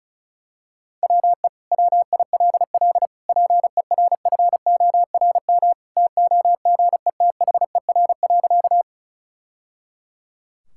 Morse code, heard as a keyed tone, about 34 wpm, 700 Hz. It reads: WE WILL PERFORM TOGETHER.
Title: Sentences from Top 1000 Words - Review Part 5 - 35wpm